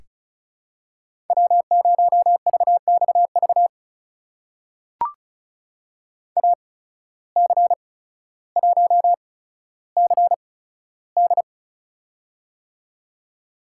Morse code, heard as a keyed tone, about 36 wpm, spaced as for 15 wpm, 700 Hz. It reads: W0VXV AC1CD
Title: USA Call Signs - 5 Characters - 8x Character Spacing - 35wpm